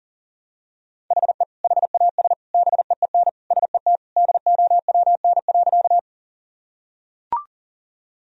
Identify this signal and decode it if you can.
Morse code, heard as a keyed tone, about 40 wpm, 700 Hz. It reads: HE HAS BEEN SET DOWN.
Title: Sentences from Top 1000 Words - Review Part 1 - 40wpm